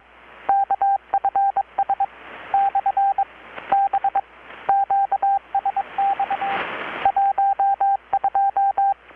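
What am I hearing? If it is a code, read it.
Morse code, heard as a keyed tone, about 22 wpm, 790 Hz, with radio noise. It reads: KFS / B QSX 12
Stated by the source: Morse from KPH's Night of Nights XX maritime radiotelegraphy event, received in Chiapas, Mexico about 3,800 km away on a Kenwood R-600 with a grounded 10 m long-wire antenna.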